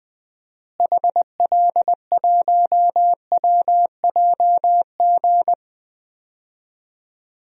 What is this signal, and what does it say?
Morse code, heard as a keyed tone, about 20 wpm, 700 Hz. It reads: HL1WJG
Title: DXCC-100 Call Signs - 6 Characters - Repeated 2x - 20wpm